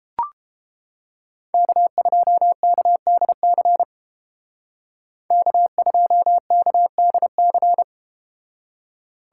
Morse code, heard as a keyed tone, about 31 wpm, 700 Hz. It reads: K2KDC K2KDC